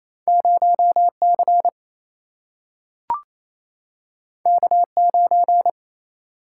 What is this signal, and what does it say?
Morse code, heard as a keyed tone, about 28 wpm, 700 Hz. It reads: 0 C K 9
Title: Single-Letter Single-Number 28wpm